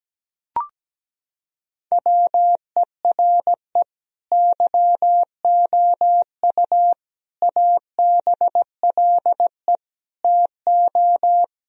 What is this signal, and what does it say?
Morse code, heard as a keyed tone, about 17 wpm, 700 Hz. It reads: WERE YOU ABLE TO
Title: Sentences from Top 1000 Words - Review Part 4 - 17wpm